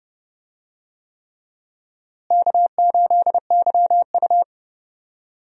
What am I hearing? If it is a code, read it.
Morse code, heard as a keyed tone, about 30 wpm, 700 Hz. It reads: K8YU
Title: USA Call Signs - 4 Characters - Repeated 3x - 30wpm